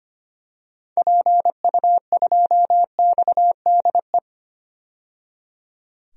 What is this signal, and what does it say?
Morse code, heard as a keyed tone, about 25 wpm, 700 Hz. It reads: PU2XDE
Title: DXCC-100 Call Signs - 6 Characters - 25wpm